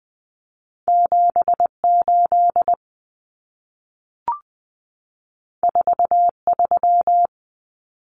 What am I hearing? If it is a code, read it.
Morse code, heard as a keyed tone, about 20 wpm, 700 Hz. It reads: 78 43
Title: Common Cut Numbers - 2-Digit 20wpm